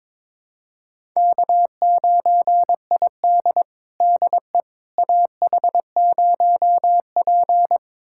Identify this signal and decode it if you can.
Morse code, heard as a keyed tone, about 22 wpm, 700 Hz. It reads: K9ID DE AH0P
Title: USA Callsign DE Callsign - 4 Characters - 22wpm